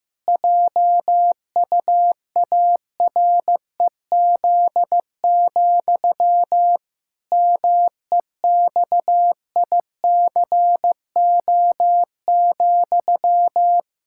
Morse code, heard as a keyed tone, about 16 wpm, 700 Hz. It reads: JUAREZ, MEXICO,
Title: Pangram Challenge v2 - 15wpm